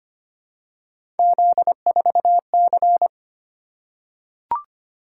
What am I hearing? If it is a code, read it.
Morse code, heard as a keyed tone, about 25 wpm, 700 Hz. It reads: Z4C
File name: Three Letter-Number Combinations - 25wpm